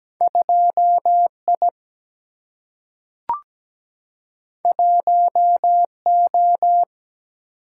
Morse code, heard as a keyed tone, about 17 wpm, 700 Hz. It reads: 2I 1O
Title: Single-Letter Single-Number 17wpm